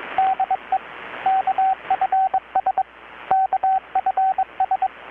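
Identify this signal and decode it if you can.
Morse code, heard as a keyed tone, about 21 wpm, 760 Hz, with radio noise. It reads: DE KFS KFS
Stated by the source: Morse from KPH's Night of Nights XX maritime radiotelegraphy event, received in Chiapas, Mexico about 3,800 km away on a Kenwood R-600 with a grounded 10 m long-wire antenna.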